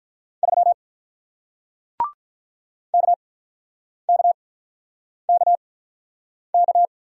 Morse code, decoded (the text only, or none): 3 KKKK